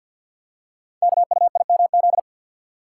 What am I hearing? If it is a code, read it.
Morse code, heard as a keyed tone, about 48 wpm, 690 Hz. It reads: KUIMZ
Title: Code Groups - 5 Letters - 50wpm